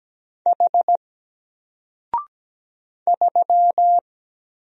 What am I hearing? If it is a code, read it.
Morse code, heard as a keyed tone, about 17 wpm, 700 Hz. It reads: H 3
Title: Single Character (Letter:Number:Punctuation:Prosign) 17wpm